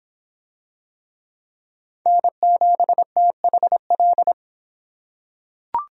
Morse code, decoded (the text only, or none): N7THL